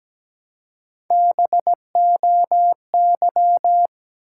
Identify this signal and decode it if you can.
Morse code, heard as a keyed tone, about 17 wpm, 700 Hz. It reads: BOY
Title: Three Letter Words - 17wpm